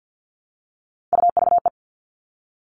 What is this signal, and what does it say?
Morse code, heard as a keyed tone, about 50 wpm, 700 Hz. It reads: UVE
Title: All Cut Numbers - 3-Digit 50wpm